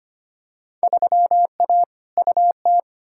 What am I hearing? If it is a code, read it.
Morse code, heard as a keyed tone, about 25 wpm, 700 Hz. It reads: 3A UT
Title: Field Day Exchanges - 25wpm